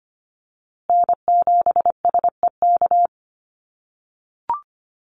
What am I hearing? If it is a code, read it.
Morse code, heard as a keyed tone, about 25 wpm, 700 Hz. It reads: N7SEK
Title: USA Call Signs Repeated 2x 25wpm